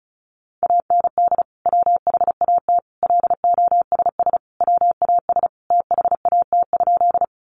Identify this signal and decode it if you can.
Morse code, heard as a keyed tone, about 34 wpm, 700 Hz. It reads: AND WHAT LOSS WAS THAT?
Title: Sentences from Top 1400 Words 35wpm